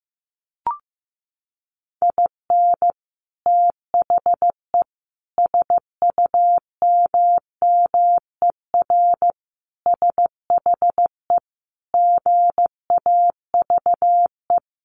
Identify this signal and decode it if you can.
Morse code, heard as a keyed tone, about 15 wpm, 700 Hz. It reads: IN THE SUMMER SHE GAVE
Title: Sentences from Top 1100 Words 15wpm